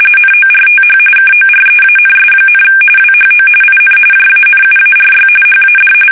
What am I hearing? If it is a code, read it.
RTTY, about 45.45 baud. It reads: NORTH KOREANS, WHO IN JANUARY HAD
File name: fsk850 / battleshipnj